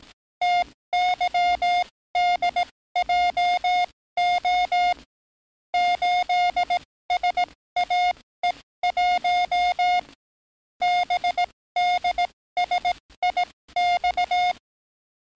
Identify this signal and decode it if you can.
Morse code, heard as a keyed tone, about 18 wpm, spaced as for 14 wpm, 710 Hz. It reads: TYDJO 8SAE1 BDSIX
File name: training / NAVPERS 11049to11060 / NAVPERS 11057 RA RB